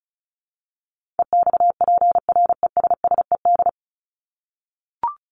Morse code, heard as a keyed tone, about 34 wpm, 700 Hz. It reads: EXPRESSED
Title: ED Suffix - Top 100 Words with Base Words 35wpm